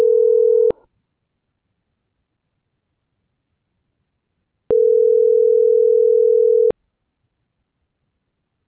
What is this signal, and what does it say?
Ringback tone (North America), 440 + 480 Hz.